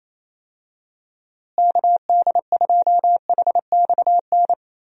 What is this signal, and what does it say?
Morse code, heard as a keyed tone, about 27 wpm, 700 Hz. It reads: KD2HXN
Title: USA Call Signs - 6 Characters - Repeated 3x - 28wpm